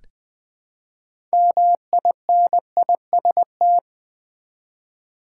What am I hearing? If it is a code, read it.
Morse code, heard as a keyed tone, about 20 wpm, 700 Hz. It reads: MINIST
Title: Build A Word - 3 Letters At A Time - 8-Letter Words - 20wpm